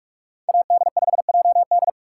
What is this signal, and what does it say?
Morse code, heard as a keyed tone, about 46 wpm, 700 Hz. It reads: AD5JD
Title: USA Call Signs - 5 Characters - 6x Character Spacing - 45wpm